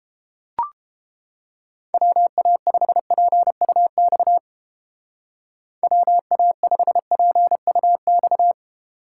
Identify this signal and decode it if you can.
Morse code, heard as a keyed tone, about 28 wpm, 700 Hz. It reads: WA5PUX WA5PUX